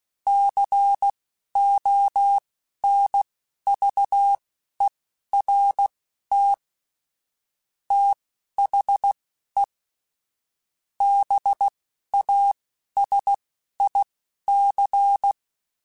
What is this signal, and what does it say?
Morse code, heard as a keyed tone, about 16 wpm, spaced as for 11 wpm, 800 Hz. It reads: CONVERT THE BASIC
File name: Technical Subjects / Technical Subjects-mp3s / Sw Pwr Sup @10